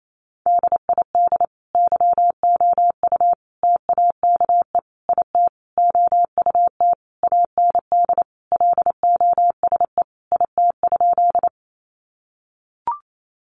Morse code, heard as a keyed tone, about 28 wpm, 700 Hz. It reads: DID YOU TAKE IT OUT AND LOSE IT?
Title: Sentences from Top 1100 Words 28wpm